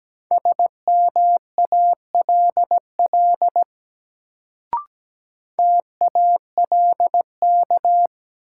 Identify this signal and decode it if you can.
Morse code, heard as a keyed tone, about 17 wpm, 700 Hz. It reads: SMALL TALK
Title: Sets of 2 Words - New Word Drill - 17wpm